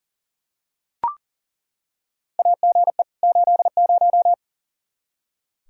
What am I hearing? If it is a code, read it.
Morse code, heard as a keyed tone, about 37 wpm, 700 Hz. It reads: AGE 80
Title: K1USN SST QSO Practice 40wpm